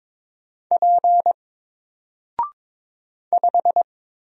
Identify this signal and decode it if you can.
Morse code, heard as a keyed tone, about 22 wpm, 700 Hz. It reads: P 5
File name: Single-Letter-Number-22wpm